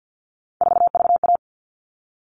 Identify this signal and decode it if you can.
Morse code, heard as a keyed tone, about 50 wpm, 700 Hz. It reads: 4VA
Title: All Cut Numbers - 3-Digit 50wpm